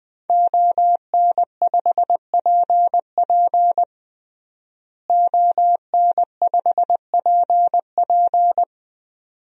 Morse code, heard as a keyed tone, about 20 wpm, 700 Hz. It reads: ON5PP ON5PP
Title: DXCC-100 Call Signs - 5 Characters - Repeated 2x - 20wpm